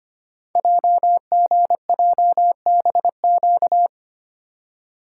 Morse code, heard as a keyed tone, about 25 wpm, 700 Hz. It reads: JGJBQ
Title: Code Groups - 5 Letters - 25wpm